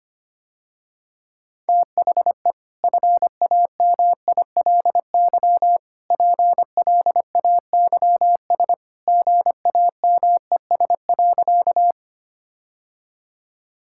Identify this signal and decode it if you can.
Morse code, heard as a keyed tone, about 25 wpm, 700 Hz. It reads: THE FAMILY PLAYS GAMES.
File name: Easy Sentences with Hints - 25wpm